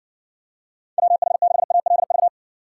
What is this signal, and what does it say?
Morse code, heard as a keyed tone, about 60 wpm, 700 Hz. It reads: WH6ALV